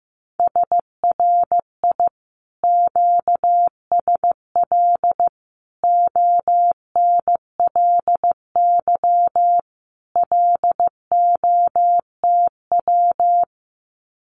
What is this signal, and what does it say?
Morse code, heard as a keyed tone, about 15 wpm, 700 Hz. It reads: SRI QSL ONLY LOTW
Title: Common QSO Phrases Encore 15wpm